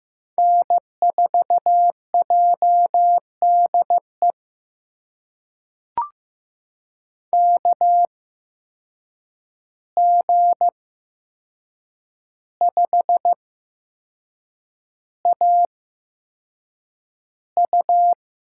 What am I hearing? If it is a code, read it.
Morse code, heard as a keyed tone, about 15 wpm, spaced as for 6 wpm, 700 Hz. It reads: N4JDE KG5AU
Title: USA Call Signs - 5 Characters - 8x Character Spacing - 15wpm